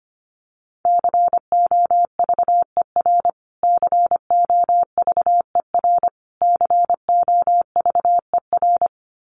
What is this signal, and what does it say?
Morse code, heard as a keyed tone, about 25 wpm, 700 Hz. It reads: COVER COVER COVER
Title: Top 500 Words Repeated 25wpm